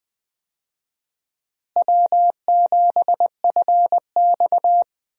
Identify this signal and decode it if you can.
Morse code, heard as a keyed tone, about 19 wpm, 700 Hz. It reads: W7FX
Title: USA Call Signs - 4 Characters - Repeated 3x - 20wpm